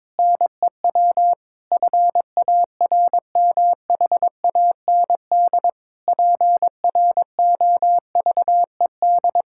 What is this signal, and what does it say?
Morse code, heard as a keyed tone, about 22 wpm, 700 Hz. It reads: NEW FARMHAND PROVED
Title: Pangram Challenge v2 - 22wpm